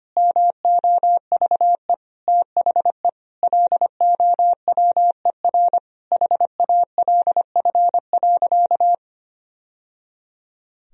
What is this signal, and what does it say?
Morse code, heard as a keyed tone, about 25 wpm, 700 Hz. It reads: MOVE THE LOWER HALF.
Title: Sentences from Top 1000 Words - Review Part 4 - 25wpm